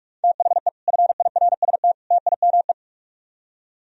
Morse code, heard as a keyed tone, about 58 wpm, 700 Hz. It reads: THE FIRST TIME